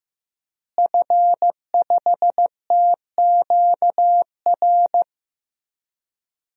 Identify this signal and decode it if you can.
Morse code, heard as a keyed tone, about 15 wpm, 700 Hz. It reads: F5TQR
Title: DXCC-100 Call Signs - 5 Characters - Repeated 3x - 15wpm